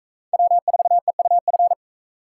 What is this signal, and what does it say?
Morse code, heard as a keyed tone, about 43 wpm, 700 Hz. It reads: W4EUF